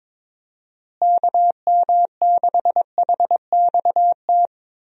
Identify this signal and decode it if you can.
Morse code, heard as a keyed tone, about 22 wpm, 700 Hz. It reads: KM6HXT
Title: USA Call Signs - 6 Characters - Repeated 2x - 22wpm